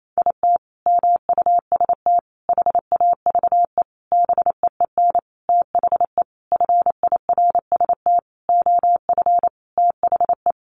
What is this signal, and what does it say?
Morse code, heard as a keyed tone, about 28 wpm, 700 Hz. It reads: IT MUST HAVE BEEN THE FIRST OF THE
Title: Sentences from Top 800 Words 28wpm